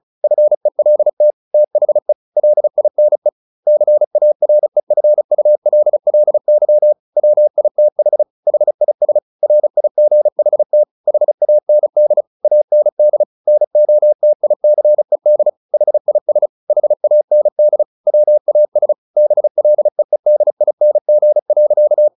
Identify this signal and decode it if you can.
Morse code, heard as a keyed tone, about 34 wpm, 600 Hz. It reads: FELT THE LINE CAREFULLY WITH HIS RIGHT HAND AND NOTICED HIS HAND WAS BLEEDING.
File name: the old man and the sea / the old man and the sea 1 35wpm